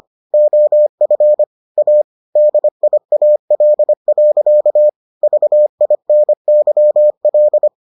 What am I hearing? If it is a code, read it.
Morse code, heard as a keyed tone, about 25 wpm, 600 Hz. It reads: OF A DIAL. VINYL